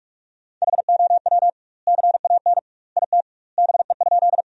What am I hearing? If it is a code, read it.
Morse code, heard as a keyed tone, about 46 wpm, 700 Hz. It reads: HOW CAN IT BE?
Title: 2x Word-Spacing - Sentences from Top 100 Words - 45wpm